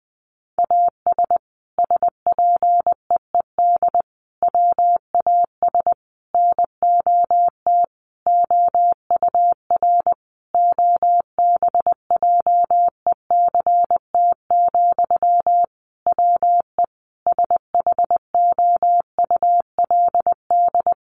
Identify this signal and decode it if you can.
Morse code, heard as a keyed tone, about 20 wpm, 700 Hz. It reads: AS SPEED WAS NOT OUR OBJECT, WE SHOULD
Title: Sentences from Top 800 Words 20wpm